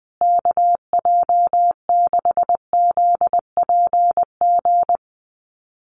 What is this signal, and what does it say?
Morse code, heard as a keyed tone, about 20 wpm, 700 Hz. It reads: KJ6ZPG
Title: USA Call Signs Repeated 3x 20wpm